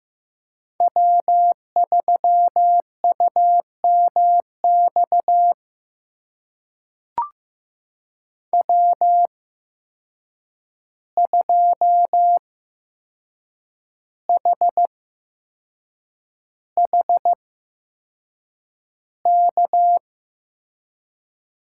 Morse code, heard as a keyed tone, about 15 wpm, spaced as for 6 wpm, 700 Hz. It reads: W3UMX W2HHK
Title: USA Call Signs - 5 Characters - 8x Character Spacing - 15wpm